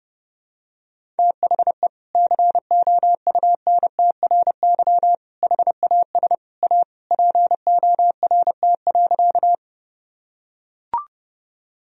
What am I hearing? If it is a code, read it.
Morse code, heard as a keyed tone, about 29 wpm, 700 Hz. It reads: THE COUNTRY HAS A PORT.
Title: Sentences from Top 1000 Words - Review Part 1 - 30wpm